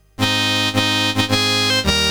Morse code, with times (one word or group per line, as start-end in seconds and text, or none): none